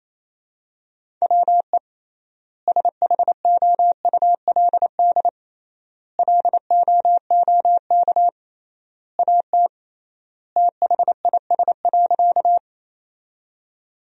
Morse code, heard as a keyed tone, about 27 wpm, 700 Hz. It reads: WE SHOULD LOOK AT THIS.
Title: 3x Word-Spacing - Sentences from Top 300 Words - 28wpm